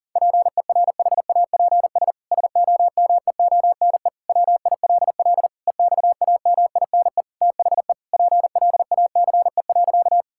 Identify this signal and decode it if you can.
Morse code, heard as a keyed tone, about 40 wpm, 700 Hz. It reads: PERHAPS SOMEONE WILL EXAMINE THE PLACE.
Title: Sentences from Top 1600 Words 40wpm